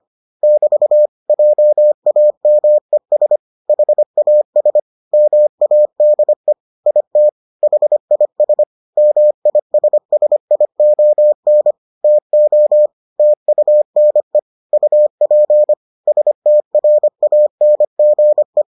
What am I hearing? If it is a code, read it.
Morse code, heard as a keyed tone, about 25 wpm, 600 Hz. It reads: <BT> JAMES HAS MADE IT HIS MISSION TO TUNE UP STRANGE